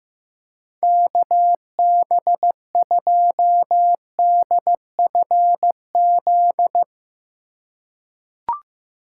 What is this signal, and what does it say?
Morse code, heard as a keyed tone, about 15 wpm, 700 Hz. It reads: KB2DFZ
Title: USA Call Signs - 2x Character Spacing - 15wpm